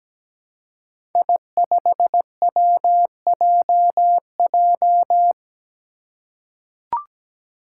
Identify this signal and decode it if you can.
Morse code, heard as a keyed tone, about 17 wpm, 700 Hz. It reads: I5WJJ
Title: Five Challenging Letter-Number Combinations 17wpm